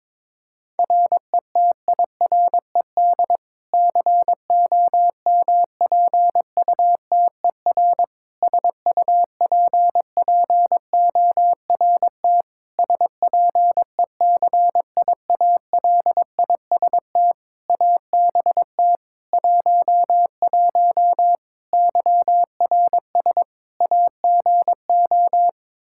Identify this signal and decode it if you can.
Morse code, heard as a keyed tone, about 22 wpm, 700 Hz. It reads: RETIRED COMPUTER SUPPORT SPECIALIST ABT 11 YRS AGO
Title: Practice QSO - Exchange 3 - More Info - 22wpm